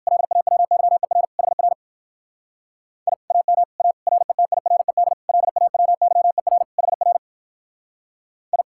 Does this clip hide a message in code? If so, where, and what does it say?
Yes: Morse code from 0.1 to 8.6 s, heard as a keyed tone, about 62 wpm, 700 Hz. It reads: LAWYER HR I AM A RETIRED LAWYER HR I